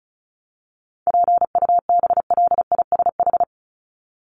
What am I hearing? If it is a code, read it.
Morse code, heard as a keyed tone, about 34 wpm, 700 Hz. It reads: PUBLISH
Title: ED Suffix - Top 100 Words with Base Words 35wpm